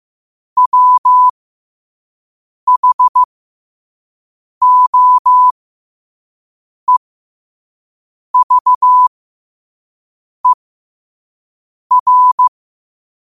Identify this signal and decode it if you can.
Morse code, heard as a keyed tone, about 15 wpm, spaced as for 5 wpm, 1000 Hz. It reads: WHOEVER
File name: qotdfiles / 5wpm